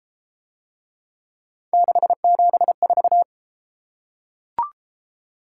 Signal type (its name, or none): Morse code